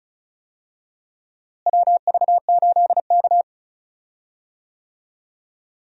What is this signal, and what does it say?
Morse code, heard as a keyed tone, about 36 wpm, 700 Hz. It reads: WV8K
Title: USA Call Signs - 4 Characters - Repeated 3x - 35wpm